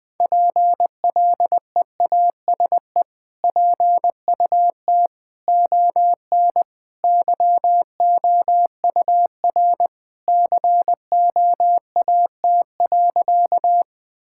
Morse code, heard as a keyed tone, about 20 wpm, 700 Hz. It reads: PLEASE PUT ON YOUR COAT.
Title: Sentences from Top 1600 Words 20wpm